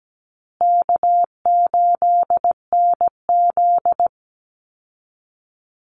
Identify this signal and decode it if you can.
Morse code, heard as a keyed tone, about 17 wpm, 700 Hz. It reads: K8NZ
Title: Easy Call Signs Repeated 3x - 17wpm